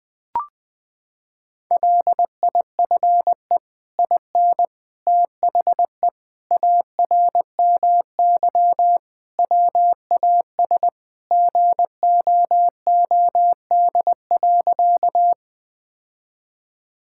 Morse code, heard as a keyed tone, about 20 wpm, 700 Hz. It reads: LIFE IN THE ARMY WAS GOOD.
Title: Sentences from Top 1000 Words - Review Part 5 - 20wpm